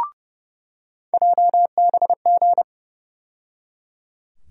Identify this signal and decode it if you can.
Morse code, heard as a keyed tone, about 30 wpm, 700 Hz. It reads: JBG
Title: USA Call Sign Postfixes - 3 Letters 30wpm